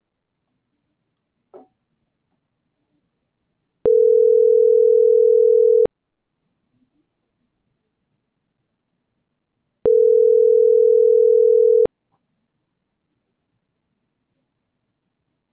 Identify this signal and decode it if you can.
Ringback tone (North America), 440 + 480 Hz.